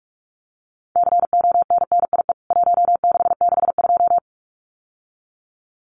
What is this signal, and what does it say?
Morse code, heard as a keyed tone, about 44 wpm, 700 Hz. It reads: CONNIE 1662
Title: CWT Exchange 45wpm